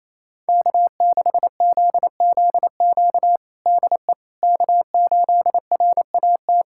Morse code, heard as a keyed tone, about 28 wpm, 700 Hz. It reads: K6ZZQ DE K8RAT
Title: USA Callsign DE Callsign 28wpm